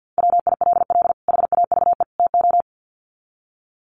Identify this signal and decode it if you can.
Morse code, heard as a keyed tone, about 50 wpm, 700 Hz. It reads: WILL HAVE TO
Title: Sets of 3 Words 50wpm